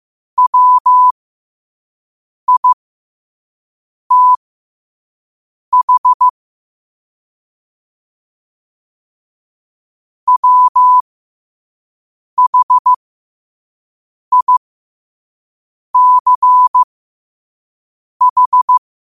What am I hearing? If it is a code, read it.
Morse code, heard as a keyed tone, about 15 wpm, spaced as for 5 wpm, 1000 Hz. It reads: WITH WHICH
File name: qotdfiles / 5wpm